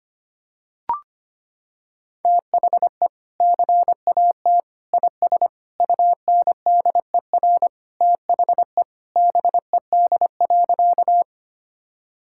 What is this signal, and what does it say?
Morse code, heard as a keyed tone, about 25 wpm, 700 Hz. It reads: THE CAT IS UNDER THE BED.
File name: practice / 25wpm / Sentences with 2-Word Phrases - 25wpm